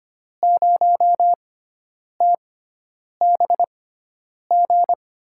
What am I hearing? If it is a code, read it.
Morse code, heard as a keyed tone, about 25 wpm, spaced as for 9 wpm, 700 Hz. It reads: 0TBG
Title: USA Call Signs - 6 Characters - 6x Character Spacing - 25wpm